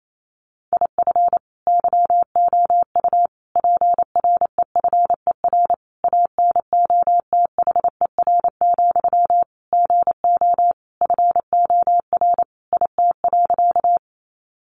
Morse code, heard as a keyed tone, about 28 wpm, 700 Hz. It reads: IF YOU PREFER ANOTHER, GO FOR IT.
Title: Sentences from Top 1100 Words 28wpm